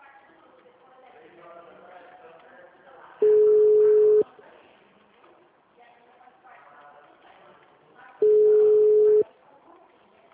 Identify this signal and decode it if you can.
Ringback tone (Europe, 425 Hz), with speech.